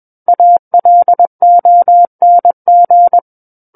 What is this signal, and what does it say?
Morse code, heard as a keyed tone, about 21 wpm, 700 Hz. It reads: ALONG
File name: Book One / 12-Chapter 12 - 21 WPM